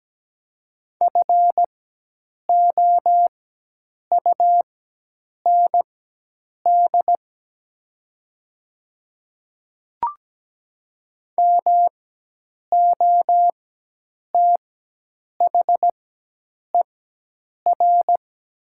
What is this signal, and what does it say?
Morse code, heard as a keyed tone, about 17 wpm, spaced as for 8 wpm, 700 Hz. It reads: FOUND MOTHER
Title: 4x Character Spacing - Top 300 Words - 17wpm